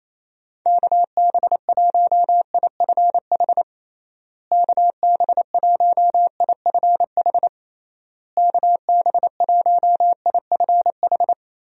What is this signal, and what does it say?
Morse code, heard as a keyed tone, about 28 wpm, 700 Hz. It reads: KB1IFH KB1IFH KB1IFH
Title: USA Call Signs - 6 Characters - Repeated 3x - 28wpm